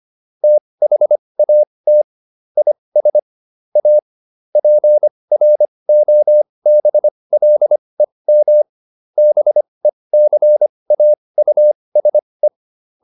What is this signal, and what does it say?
Morse code, heard as a keyed tone, about 25 wpm, spaced as for 20 wpm, 600 Hz. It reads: THAT IS A PROBLEM BECAUSE